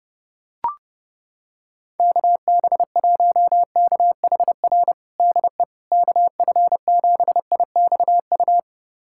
Morse code, heard as a keyed tone, about 29 wpm, 700 Hz. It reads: KB1KHR DE KF7IXU
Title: USA Callsign DE Callsign - 6 Characters - 30wpm